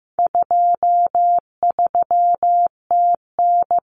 Morse code, heard as a keyed tone, about 15 wpm, 700 Hz. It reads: 23TN
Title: Common Cut Numbers - 4-Digit 15wpm